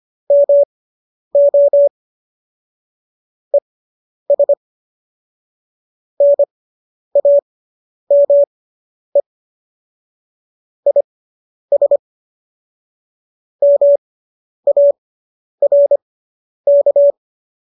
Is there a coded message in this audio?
Yes: Morse code from 0.3 to 17.1 s, heard as a keyed tone, about 25 wpm, spaced as for 10 wpm, 570 Hz. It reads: MO ES NAME IS MARK